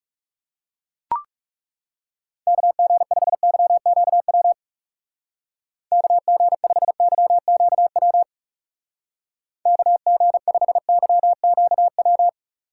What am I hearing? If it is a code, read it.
Morse code, heard as a keyed tone, about 36 wpm, 700 Hz. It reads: KG5YQW KG5YQW KG5YQW
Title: Speed Racing - USA 2x3 Call Signs 30wpm